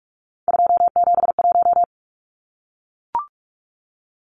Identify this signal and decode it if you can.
Morse code, heard as a keyed tone, about 44 wpm, 700 Hz. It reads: 271
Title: Numbers - 3-Digit 45wpm